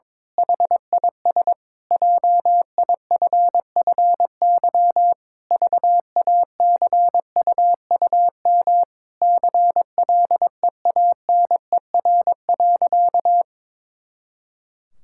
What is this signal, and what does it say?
Morse code, heard as a keyed tone, about 22 wpm, 700 Hz. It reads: HIS JIFFY VACUUM CLEANER.
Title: Pangram Challenge v2 - 22wpm